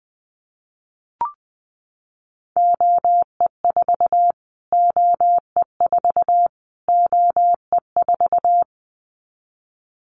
Morse code, heard as a keyed tone, about 20 wpm, 700 Hz. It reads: OE4 OE4 OE4
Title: Callsign Prefixes for Top 25 DXCC - Repeated 3x - 20wpm